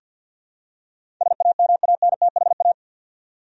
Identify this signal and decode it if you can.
Morse code, heard as a keyed tone, about 47 wpm, 700 Hz. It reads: SAMANTHA
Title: USA Common Names - 50wpm